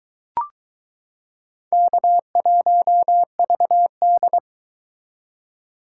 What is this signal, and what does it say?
Morse code, heard as a keyed tone, about 23 wpm, 700 Hz. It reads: K1VD